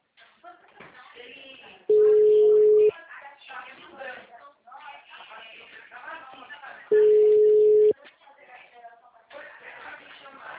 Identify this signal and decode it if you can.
Ringback tone (Europe, 425 Hz), with speech.